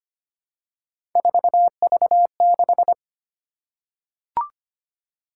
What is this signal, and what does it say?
Morse code, heard as a keyed tone, about 25 wpm, 700 Hz. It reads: 4V6
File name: Three Letter-Number Combinations - 25wpm